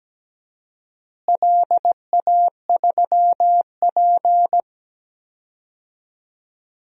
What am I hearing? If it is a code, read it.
Morse code, heard as a keyed tone, about 17 wpm, 700 Hz. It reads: LA3P
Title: DXCC-100 Call Signs - 4 Characters - Repeated 2x - 17wpm